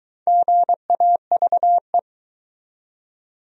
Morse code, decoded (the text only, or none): GAVE